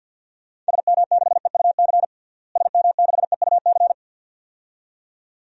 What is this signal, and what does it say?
Morse code, heard as a keyed tone, about 50 wpm, 700 Hz. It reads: SM6EUC SM6EUC
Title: DXCC-100 Call Signs - 6 Characters - Repeated 2x - 50wpm